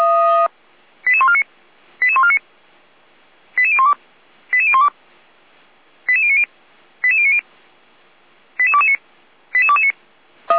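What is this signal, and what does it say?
Five-tone selective call (ZVEI), heard as quick runs of five tones, tones of about 70 ms. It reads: 88149 88149 88012 88012 88090 88090 88229 88229
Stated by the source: ZVEI five-tone call 88192 received on an SDR.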